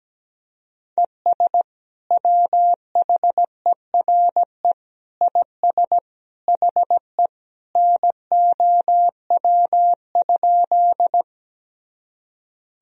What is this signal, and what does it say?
Morse code, heard as a keyed tone, about 17 wpm, 700 Hz. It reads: ES WHERE IS HE NOW?
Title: Sentences from Top 200 Words and Abbreviations 17wpm